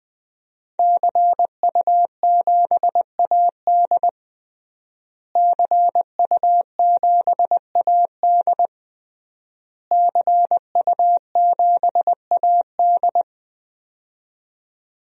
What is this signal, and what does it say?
Morse code, heard as a keyed tone, about 20 wpm, 700 Hz. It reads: CU7AD CU7AD CU7AD
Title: DXCC-100 Call Signs - 5 Characters - Repeated 3x - 20wpm